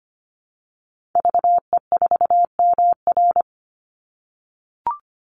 Morse code, heard as a keyed tone, about 25 wpm, 700 Hz. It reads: VE4MR
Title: Call Signs - 25wpm